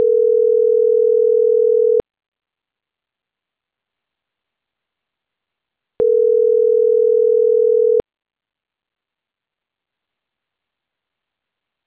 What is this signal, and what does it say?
Ringback tone (North America), 440 + 480 Hz.